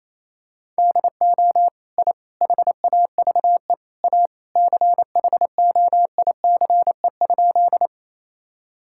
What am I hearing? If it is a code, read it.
Morse code, heard as a keyed tone, about 28 wpm, 700 Hz. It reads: DO I HAVE A CHOICE?
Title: Sentences from Top 1000 Words - Review Part 6 - 28wpm